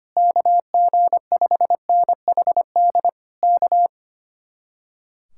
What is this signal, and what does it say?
Morse code, heard as a keyed tone, about 24 wpm, 700 Hz. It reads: KG5NHD K
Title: Speed-Racing Common QSO Phrases - 25wpm